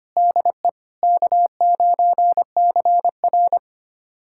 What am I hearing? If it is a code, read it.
Morse code, heard as a keyed tone, about 25 wpm, 700 Hz. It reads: DE K9CR
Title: USA Callsign DE Callsign - 4 Characters - 25wpm